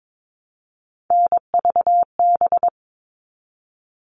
Morse code, heard as a keyed tone, about 22 wpm, 700 Hz. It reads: NVB
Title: All Cut Numbers - 3-Digit 22wpm